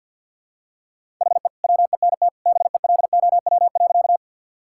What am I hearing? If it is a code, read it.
Morse code, heard as a keyed tone, about 52 wpm, 700 Hz. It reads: HE WENT BELOW.
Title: Sentences from Top 1000 Words - Review Part 5 - 50wpm